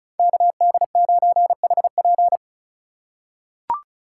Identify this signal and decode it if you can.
Morse code, heard as a keyed tone, about 36 wpm, 700 Hz. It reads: KD9HP
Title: USA Call Signs - 5 Characters - 4x Character Spacing - 35wpm